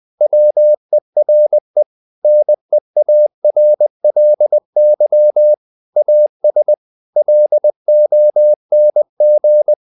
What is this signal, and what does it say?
Morse code, heard as a keyed tone, about 20 wpm, 600 Hz. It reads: WERE NEARLY AS LONG
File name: the old man and the sea part 2 20wpm